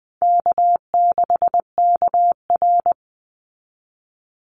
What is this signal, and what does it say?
Morse code, heard as a keyed tone, about 20 wpm, 700 Hz. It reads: K6KR
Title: Easy Call Signs Repeated 3x - 20wpm